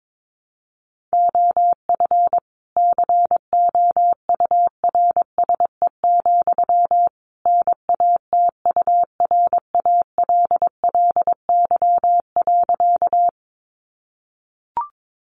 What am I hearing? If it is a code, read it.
Morse code, heard as a keyed tone, about 22 wpm, 700 Hz. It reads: OF COURSE, NATURALLY.
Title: Sentences from Top 1300 Words 22wpm